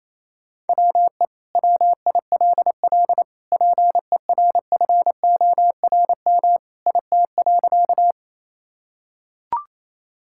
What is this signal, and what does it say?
Morse code, heard as a keyed tone, about 28 wpm, 700 Hz. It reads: WE WILL PERFORM IT.
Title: Sentences from Top 1000 Words - Review Part 5 - 28wpm